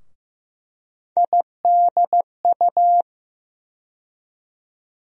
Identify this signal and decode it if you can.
Morse code, heard as a keyed tone, about 15 wpm, 700 Hz. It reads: IDU
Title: USA Call Sign Postfixes - 3 Letters 15wpm